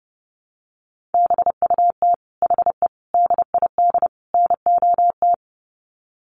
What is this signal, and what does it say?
Morse code, heard as a keyed tone, about 30 wpm, 700 Hz. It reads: BUT HE DID NOT
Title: Sets of 4 Words 30wpm